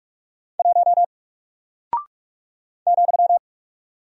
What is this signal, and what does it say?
Morse code, heard as a keyed tone, about 45 wpm, 700 Hz. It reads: 1 ,
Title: Single Character (Letter:Number:Punctuation:Prosign) Alt 45wpm